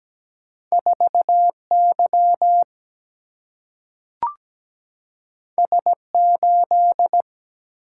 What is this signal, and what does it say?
Morse code, heard as a keyed tone, about 17 wpm, 700 Hz. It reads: 4 Y S 8
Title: Single-Letter Single-Number 17wpm